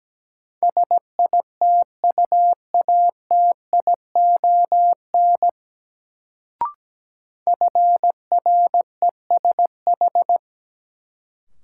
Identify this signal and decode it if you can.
Morse code, heard as a keyed tone, about 17 wpm, 700 Hz. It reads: SITUATION FRESH
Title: Sentences from Top 1200 Words - New Word Drill - 17wpm